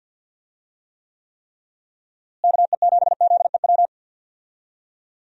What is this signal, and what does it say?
Morse code, heard as a keyed tone, about 51 wpm, 700 Hz. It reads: KE7ZEW
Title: USA Call Signs - 6 Characters - Repeated 3x - 50wpm